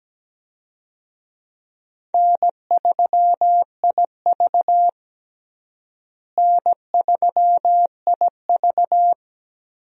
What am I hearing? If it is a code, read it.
Morse code, heard as a keyed tone, about 17 wpm, 700 Hz. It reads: N3IV N3IV
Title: USA Call Signs - 4 Characters - Repeated 2x - 17wpm